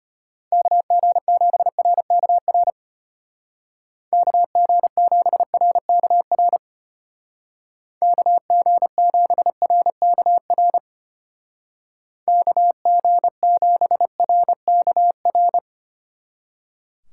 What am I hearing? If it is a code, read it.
Morse code, heard as a keyed tone, about 32 wpm, 700 Hz. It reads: KG7RKR KG7RKR KG7RKR KG7RKR